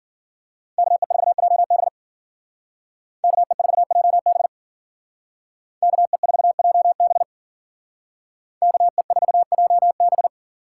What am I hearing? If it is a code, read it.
Morse code, heard as a keyed tone, about 49 wpm, 700 Hz. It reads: KE4JB KE4JB KE4JB KE4JB
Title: Speed Racing - USA 2x2 Call Signs 40wpm